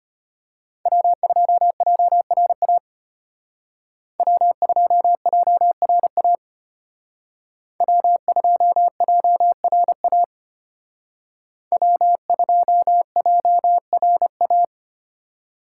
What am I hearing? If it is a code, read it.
Morse code, heard as a keyed tone, about 31 wpm, 700 Hz. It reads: W2JRA W2JRA W2JRA W2JRA
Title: Speed Racing - USA 1x3 Call Signs 25wpm